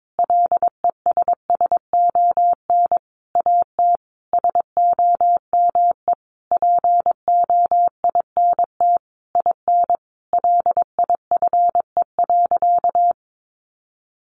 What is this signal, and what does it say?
Morse code, heard as a keyed tone, about 22 wpm, 700 Hz. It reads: LESSON AT SOME POINT IN LIFE.
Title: Sentences from Top 1200 Words 22wpm